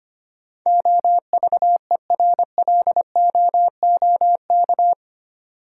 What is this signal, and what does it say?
Morse code, heard as a keyed tone, about 25 wpm, 700 Hz. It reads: OVERLOOK
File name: 8 Letter Words - First 3 Letters Given - 25wpm